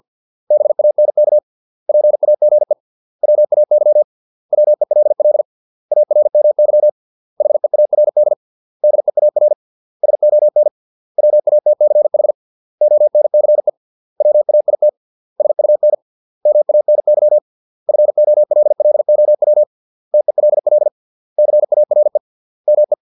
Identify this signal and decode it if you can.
Morse code, heard as a keyed tone, about 50 wpm, 600 Hz. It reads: BANK PAGE WAY WELL ARMY HEARD DEAR SON WATCH ONCE WAIT SUN MANY FOLLOW TELL CARE ME
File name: 632 words / 632 Words 50WPM